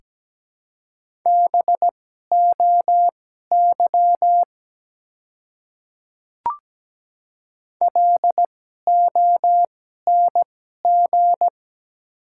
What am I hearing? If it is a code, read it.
Morse code, heard as a keyed tone, about 17 wpm, spaced as for 12 wpm, 700 Hz. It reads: BOY LONG
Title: 2x Character Spacing - Top 300 Words - 17wpm